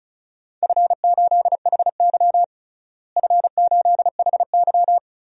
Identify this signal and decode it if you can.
Morse code, heard as a keyed tone, about 36 wpm, 700 Hz. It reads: F8HY F8HY
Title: DXCC-100 Call Signs - 4 Characters - Repeated 3x - 35wpm